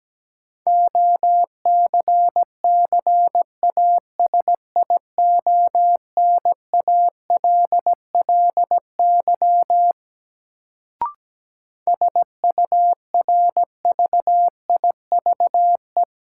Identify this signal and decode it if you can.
Morse code, heard as a keyed tone, about 17 wpm, 700 Hz. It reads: OCCASIONALLY SURVIVE
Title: Sentences from Top 1300 Words - New Word Drill - 17wpm